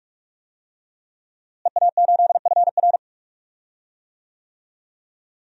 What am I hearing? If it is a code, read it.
Morse code, heard as a keyed tone, about 42 wpm, 700 Hz. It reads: EA8FR
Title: DXCC-100 Call Signs - 5 Characters - Repeated 3x - 45wpm